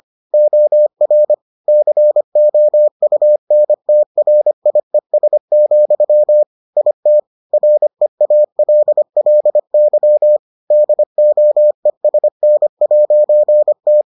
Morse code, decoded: OR COUNTRIES, IT REALLY DOESN'T

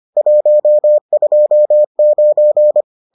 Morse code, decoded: 129

